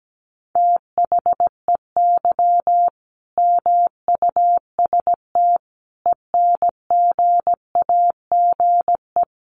Morse code, heard as a keyed tone, about 17 wpm, 700 Hz. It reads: THEY MUST ENGAGE
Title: Sentences from Top 1500 Words 17wpm